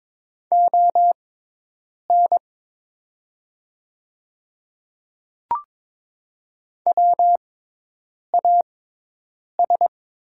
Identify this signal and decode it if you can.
Morse code, heard as a keyed tone, about 22 wpm, spaced as for 8 wpm, 700 Hz. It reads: ON WAS
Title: 6x Character Spacing - Top 100 Words - 22wpm